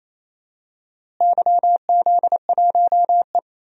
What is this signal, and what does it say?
Morse code, heard as a keyed tone, about 28 wpm, 700 Hz. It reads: YZ1E
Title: DXCC-100 Call Signs - 4 Characters - Repeated 3x - 28wpm